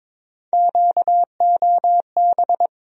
Morse code, heard as a keyed tone, about 22 wpm, 700 Hz. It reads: QOB
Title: USA Call Sign Postfixes - 3 Letters 22wpm